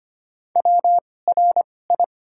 Morse code, heard as a keyed tone, about 25 wpm, spaced as for 18 wpm, 700 Hz. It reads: WRI
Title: 2x Character Spacing - Top 200 Words - 25wpm